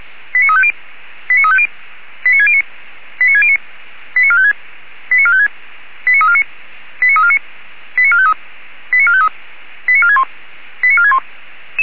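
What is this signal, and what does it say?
Five-tone selective call (ZVEI), heard as quick runs of five tones, tones of about 70 ms. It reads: 89350 89350 89609 89609 89456 89456 89349 89349 89453 89453 89461 89461
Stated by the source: ZVEI five-tone selective call received on SDR.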